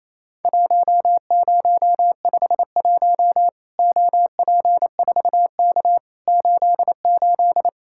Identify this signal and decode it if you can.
Morse code, heard as a keyed tone, about 28 wpm, 700 Hz. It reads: 1051 OP4K 88
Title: Worked All Europe DX Contest - QTC Exchange - 28wpm